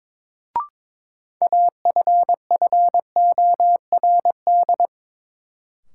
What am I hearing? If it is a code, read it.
Morse code, heard as a keyed tone, about 21 wpm, 700 Hz. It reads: AFFORD
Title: Sentences from Top 1100 Words - New Word Drill - 22wpm